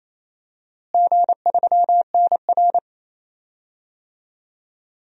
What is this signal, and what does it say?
Morse code, heard as a keyed tone, about 27 wpm, 700 Hz. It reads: G3NR
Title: DXCC-100 Call Signs - 4 Characters - Repeated 2x - 28wpm